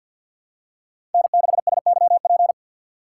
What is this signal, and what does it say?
Morse code, heard as a keyed tone, about 53 wpm, 700 Hz. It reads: N6SYP
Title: USA Call Signs - 5 Characters - 4x Character Spacing - 50wpm